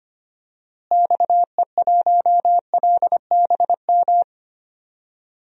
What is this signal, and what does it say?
Morse code, heard as a keyed tone, about 25 wpm, 700 Hz. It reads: XE1LBM
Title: DXCC-100 Call Signs - 6 Characters - Repeated 2x - 25wpm